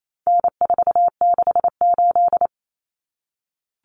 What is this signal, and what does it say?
Morse code, heard as a keyed tone, about 28 wpm, 700 Hz. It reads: N468
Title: Common Cut Numbers - 4-Digit 28wpm